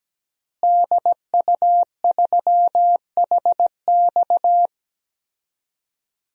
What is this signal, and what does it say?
Morse code, heard as a keyed tone, about 17 wpm, 700 Hz. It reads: DU3HX
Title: DXCC-100 Call Signs - 5 Characters - Repeated 2x - 17wpm